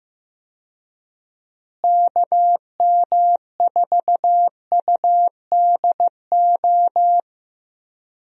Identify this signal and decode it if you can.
Morse code, heard as a keyed tone, about 15 wpm, 700 Hz. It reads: KM4UDO
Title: USA Call Signs - 6 Characters - Repeated 2x - 15wpm